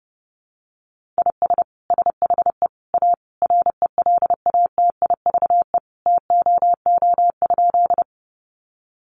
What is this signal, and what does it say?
Morse code, heard as a keyed tone, about 30 wpm, 700 Hz. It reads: IS SHE A RELATIVE TOO?
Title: Sentences from Top 1400 Words 30wpm